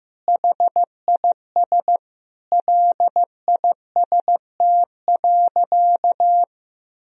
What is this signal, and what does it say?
Morse code, heard as a keyed tone, about 15 wpm, 700 Hz. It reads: HIS LIST.
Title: Sentences from Top 1000 Words - Review Part 2 - 15wpm